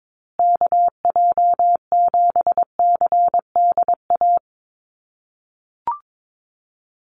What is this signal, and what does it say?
Morse code, heard as a keyed tone, about 22 wpm, 700 Hz. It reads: KJ7CDA
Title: USA Call Signs Repeated 2x 22wpm